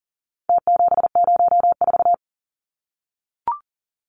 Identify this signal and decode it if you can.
Morse code, heard as a keyed tone, about 40 wpm, 700 Hz. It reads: T704
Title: Numbers - Contest Style - 4-Digit 40wpm